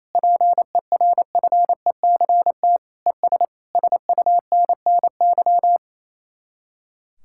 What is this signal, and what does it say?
Morse code, heard as a keyed tone, about 35 wpm, 700 Hz. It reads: PERFECT ES SUNNY
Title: Speed-Racing Common QSO Phrases Encore - 28wpm